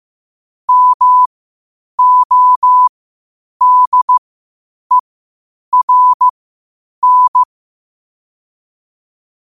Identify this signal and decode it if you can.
Morse code, heard as a keyed tone, about 15 wpm, spaced as for 8 wpm, 1000 Hz. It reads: MODERN
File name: qotdfiles / 8wpm